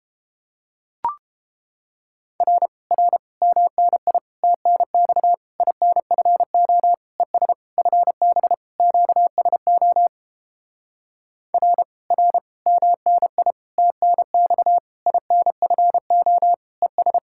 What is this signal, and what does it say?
Morse code, heard as a keyed tone, about 31 wpm, 700 Hz. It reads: R R MNI TNX INFO ES FB QSO R R MNI TNX INFO ES